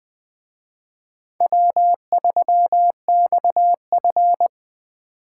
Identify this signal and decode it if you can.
Morse code, heard as a keyed tone, about 20 wpm, 700 Hz. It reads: W3XF
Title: Speed Racing - USA 1x2 Call Signs 17wpm